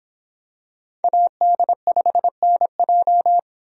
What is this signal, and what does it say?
Morse code, heard as a keyed tone, about 26 wpm, 700 Hz. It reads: AD5NJ